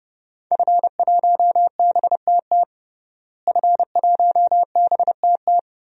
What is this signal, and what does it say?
Morse code, heard as a keyed tone, about 29 wpm, 700 Hz. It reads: F1BTT F1BTT
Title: DXCC-100 Call Signs - 5 Characters - Repeated 3x - 30wpm